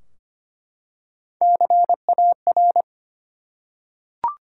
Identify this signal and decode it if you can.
Morse code, heard as a keyed tone, about 25 wpm, 700 Hz. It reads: CAR
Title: Three Letter Words - 25wpm